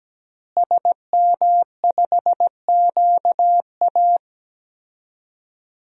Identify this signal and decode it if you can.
Morse code, heard as a keyed tone, about 17 wpm, 700 Hz. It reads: SM5QA
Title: DXCC-100 Call Signs - 5 Characters - Repeated 2x - 17wpm